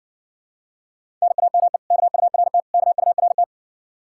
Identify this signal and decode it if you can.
Morse code, heard as a keyed tone, about 60 wpm, 700 Hz. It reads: NAME KURT KURT